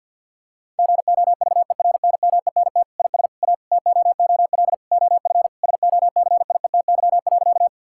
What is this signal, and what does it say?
Morse code, heard as a keyed tone, about 50 wpm, 700 Hz. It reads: GOVERNMENT IS A TOOL OF SOCIETY.
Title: Sentences from Top 1000 Words - Review Part 6 - 50wpm